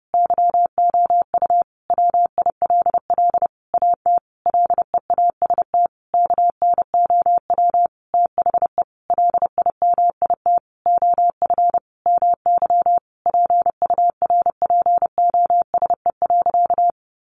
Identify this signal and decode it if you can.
Morse code, heard as a keyed tone, about 30 wpm, 700 Hz. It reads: YOU WILL AT LEAST KNOW THE LIMIT OF MY PURPOSE.
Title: Sentences from Top 1500 Words 30wpm